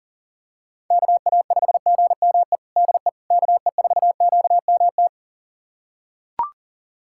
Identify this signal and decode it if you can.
Morse code, heard as a keyed tone, about 37 wpm, 700 Hz. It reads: KA5GME DE KE4QMT
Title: USA Callsign DE Callsign - 6 Characters - 40wpm